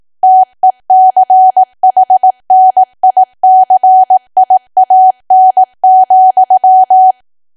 Morse code, heard as a keyed tone, about 18 wpm, 750 Hz. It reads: TECHNICIAN,
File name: cwa / W1AW-Level2 / 18WPM